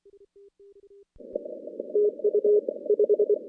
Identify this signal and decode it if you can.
Morse code, heard as a keyed tone, about 27 wpm, 410 Hz. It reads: TU 5
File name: contests / CW-SS-k5tr- -2300